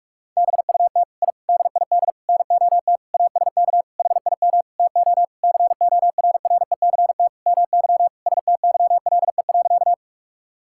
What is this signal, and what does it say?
Morse code, heard as a keyed tone, about 46 wpm, 700 Hz. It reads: BUT I DID NOT ASK HIM TO CORRECT MY STYLE.
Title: Sentences from Top 1000 Words - Review Part 5 - 45wpm